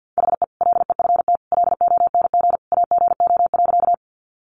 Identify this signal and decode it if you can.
Morse code, heard as a keyed tone, about 50 wpm, 700 Hz. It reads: HE LEFT LONG AGO.
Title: Sentences from Top 500 Words 50wpm